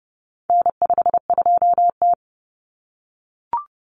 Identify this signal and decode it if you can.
Morse code, heard as a keyed tone, about 30 wpm, 700 Hz. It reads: N52T
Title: Common Cut Numbers - 4-Digit 30wpm